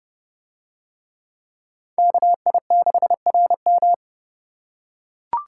Morse code, heard as a keyed tone, about 28 wpm, 700 Hz. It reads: KI6RM